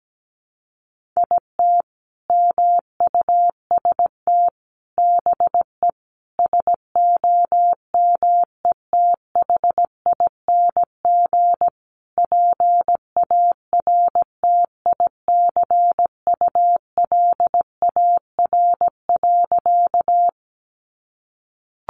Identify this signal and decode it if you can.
Morse code, heard as a keyed tone, about 17 wpm, 700 Hz. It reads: IT MUST BE SOMETHING PARTICULAR.
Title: Sentences from Top 600 Words 17wpm